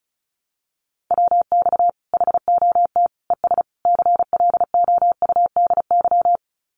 Morse code, heard as a keyed tone, about 34 wpm, 700 Hz. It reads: WX HOT ES CLOUDY
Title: Common QSO Phrases 35wpm